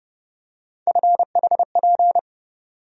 Morse code, heard as a keyed tone, about 30 wpm, 700 Hz. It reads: FHP
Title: USA Call Sign Postfixes - 3 Letters 30wpm